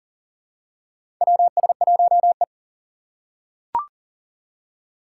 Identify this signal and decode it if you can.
Morse code, heard as a keyed tone, about 40 wpm, 700 Hz. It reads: WS1E
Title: USA Call Signs - 4 Characters - 2x Character Spacing - 40wpm